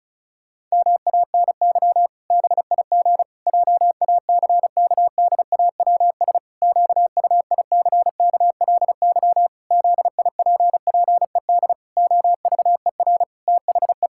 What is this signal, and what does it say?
Morse code, heard as a keyed tone, about 36 wpm, 700 Hz. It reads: MANY BIG JACKDAWS QUICKLY ZIPPED OVER THE
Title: Pangram Challenge v2 - 35wpm